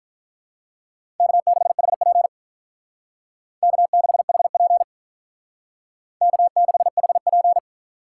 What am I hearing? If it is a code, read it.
Morse code, heard as a keyed tone, about 43 wpm, 700 Hz. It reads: K6HP K6HP K6HP
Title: Speed Racing - USA 1x2 Call Signs 35wpm